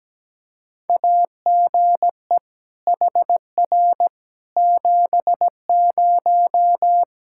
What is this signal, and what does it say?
Morse code, heard as a keyed tone, about 17 wpm, 700 Hz. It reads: AGE HR 70
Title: Practice QSO - Exchange 3 - More Info - 17wpm